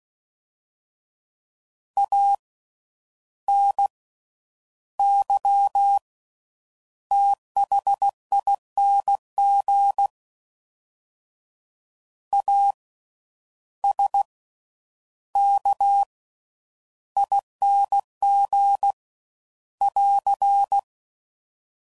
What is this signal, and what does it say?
Morse code, heard as a keyed tone, about 16 wpm, 800 Hz. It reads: A N Y THING A S K ING <AR>
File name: Suffixes / Special INGs / Slow With Faster Ings